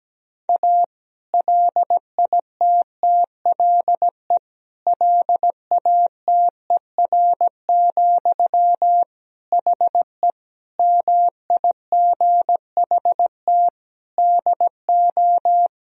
Morse code, heard as a keyed tone, about 17 wpm, 700 Hz. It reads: A LITTLE LATER, HE MIGHT DO
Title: Sentences from Top 1000 Words - Review Part 4 - 17wpm